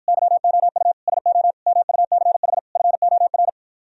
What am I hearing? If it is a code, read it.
Morse code, heard as a keyed tone, about 53 wpm, 700 Hz. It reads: YOU SO MUCH FOR